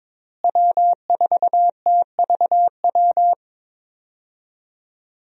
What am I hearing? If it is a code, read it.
Morse code, heard as a keyed tone, about 21 wpm, 700 Hz. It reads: W4TVW
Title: USA Call Signs - 5 Characters - Repeated 3x - 22wpm